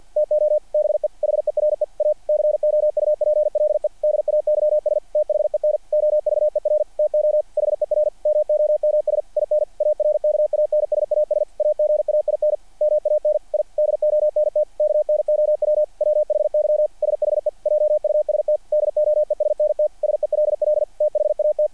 Morse code, heard as a keyed tone, about 50 wpm, 600 Hz. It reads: TO BE HERE A COUPLE DAYS THEN OVER TO HER MOMS IN ARKANSAS AGAIN MAN I DONT KNOW WHY SHE JUST DOESNT SELL THAT
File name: archived / CRSnet